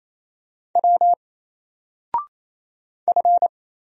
Morse code, heard as a keyed tone, about 27 wpm, 700 Hz. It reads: W F
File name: Single-Letter-Number-28wpm